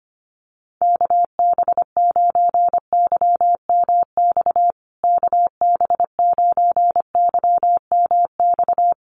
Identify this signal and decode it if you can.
Morse code, heard as a keyed tone, about 25 wpm, 700 Hz. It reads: KB9YMX KB9YMX
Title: USA Call Signs Repeated 3x 25wpm